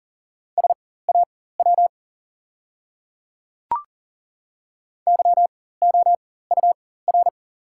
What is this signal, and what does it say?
Morse code, heard as a keyed tone, about 40 wpm, spaced as for 19 wpm, 690 Hz. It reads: SAW YOUR
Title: 4x Character Spacing - Top 300 Words - 40wpm